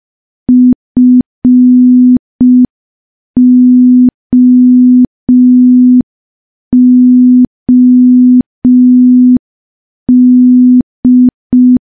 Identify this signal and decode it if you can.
Morse code, heard as a keyed tone, about 5 wpm, 250 Hz. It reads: FOOD
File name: morse / morse-training-05WPM